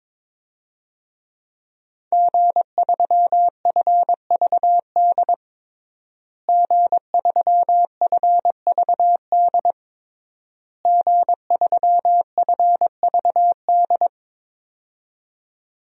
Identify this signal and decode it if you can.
Morse code, heard as a keyed tone, about 22 wpm, 700 Hz. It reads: G3FVD G3FVD G3FVD
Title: DXCC-100 Call Signs - 5 Characters - Repeated 3x - 22wpm